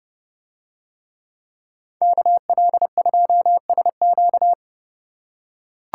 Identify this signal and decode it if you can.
Morse code, heard as a keyed tone, about 30 wpm, 700 Hz. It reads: KL2SQ